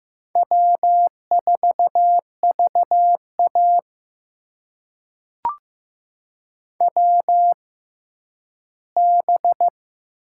Morse code, heard as a keyed tone, about 15 wpm, spaced as for 10 wpm, 700 Hz. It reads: W4VA WB
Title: USA Call Signs - 4 Characters - 6x Character Spacing - 15wpm